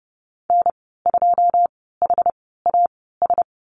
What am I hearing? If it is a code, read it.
Morse code, heard as a keyed tone, about 30 wpm, spaced as for 17 wpm, 700 Hz. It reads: N2HAS
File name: Lesson 14g - Number 2 - Call Signs 30wpm